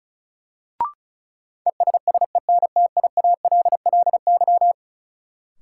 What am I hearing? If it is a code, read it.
Morse code, heard as a keyed tone, about 36 wpm, 700 Hz. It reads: ESSENTIALLY
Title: Sentences from Top 1300 Words - New Word Drill Alt - 35wpm